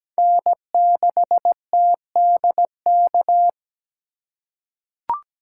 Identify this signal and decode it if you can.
Morse code, heard as a keyed tone, about 17 wpm, 700 Hz. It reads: N6TDK
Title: USA Call Signs - 5 Characters - 8x Character Spacing - 17wpm